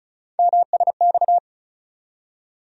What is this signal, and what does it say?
Morse code, heard as a keyed tone, about 36 wpm, 700 Hz. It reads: MSX
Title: USA Call Sign Postfixes - 3 Letters 35wpm